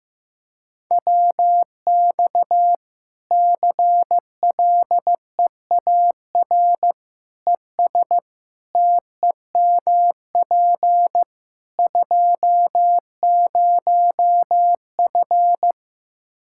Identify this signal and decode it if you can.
Morse code, heard as a keyed tone, about 15 wpm, 700 Hz. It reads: WX CLEAR ES TEMP 20F
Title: K1USN SST QSO Practice 15wpm